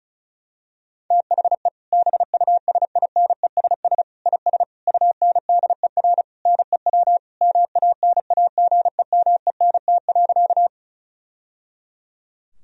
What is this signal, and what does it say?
Morse code, heard as a keyed tone, about 34 wpm, 700 Hz. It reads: THE BUSINESS IS UNDER NEW MANAGEMENT.
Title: Sentences from Top 1000 Words - Review Part 4 - 35wpm